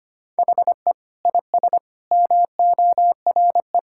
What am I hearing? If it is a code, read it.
Morse code, heard as a keyed tone, about 25 wpm, 700 Hz. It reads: HE IS MORE
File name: practice / 25wpm / Sentences with 4-Word Phrases - 25wpm